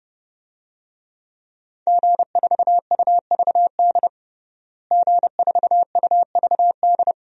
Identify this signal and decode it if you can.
Morse code, heard as a keyed tone, about 30 wpm, 700 Hz. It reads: G4UVD G4UVD
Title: DXCC-100 Call Signs - 5 Characters - Repeated 2x - 30wpm